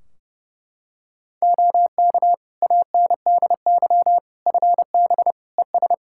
Morse code, heard as a keyed tone, about 30 wpm, 700 Hz. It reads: OK ANDY FB ES
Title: Practice QSO - Exchange 2 - Info - 30wpm